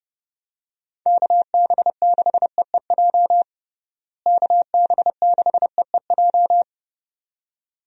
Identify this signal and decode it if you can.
Morse code, heard as a keyed tone, about 31 wpm, 700 Hz. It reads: KB6EEJ KB6EEJ
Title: USA Call Signs - 6 Characters - Repeated 2x - 30wpm